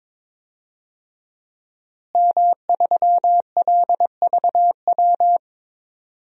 Morse code, heard as a keyed tone, about 22 wpm, 700 Hz. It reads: M3LVW
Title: DXCC-100 Call Signs - 5 Characters - Repeated 3x - 22wpm